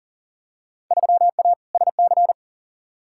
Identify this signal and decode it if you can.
Morse code, heard as a keyed tone, about 40 wpm, 700 Hz. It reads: 3A SC
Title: Field Day Exchanges Encore - 40wpm